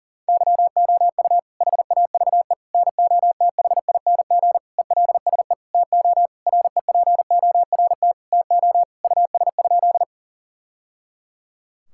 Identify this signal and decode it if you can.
Morse code, heard as a keyed tone, about 40 wpm, 700 Hz. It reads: YOU HAVE NOTHING ELSE TO REPORT TO US?
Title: Sentences from Top 1000 Words - Review Part 6 - 40wpm